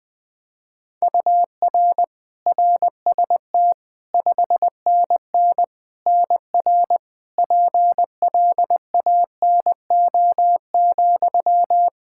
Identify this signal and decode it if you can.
Morse code, heard as a keyed tone, about 20 wpm, 700 Hz. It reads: UR RST 5NN NR PLANO,
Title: Practice QSO - Exchange 1 - Beginning - 20wpm